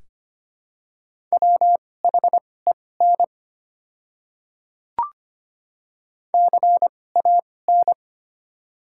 Morse code, heard as a keyed tone, about 25 wpm, spaced as for 18 wpm, 690 Hz. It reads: WHEN CAN
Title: 2x Character Spacing - Top 100 Words - 25wpm